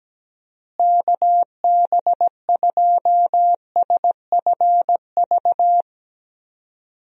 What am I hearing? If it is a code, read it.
Morse code, heard as a keyed tone, about 17 wpm, 700 Hz. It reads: KB2SFV
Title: USA Call Signs - 6 Characters - Repeated 2x - 17wpm